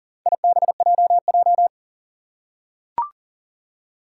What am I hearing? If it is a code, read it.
Morse code, heard as a keyed tone, about 40 wpm, 700 Hz. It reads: IBJJ
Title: Four Challenging Letter-Number Combinations 40wpm